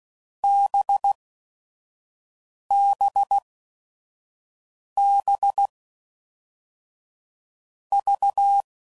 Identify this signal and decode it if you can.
Morse code, heard as a keyed tone, about 16 wpm, spaced as for 5 wpm, 800 Hz. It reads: BBB V
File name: Groups / Basic Letter Groups / VB46